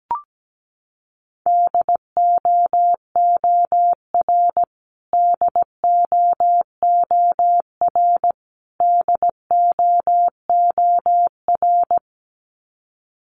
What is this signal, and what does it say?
Morse code, heard as a keyed tone, about 17 wpm, 700 Hz. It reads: DOOR DOOR DOOR
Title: Top 500 Words Repeated 17wpm